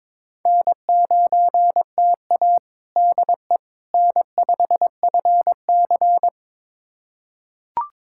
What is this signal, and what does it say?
Morse code, heard as a keyed tone, about 22 wpm, 700 Hz. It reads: N9TA DE N5FC
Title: USA Callsign DE Callsign - 4 Characters - 22wpm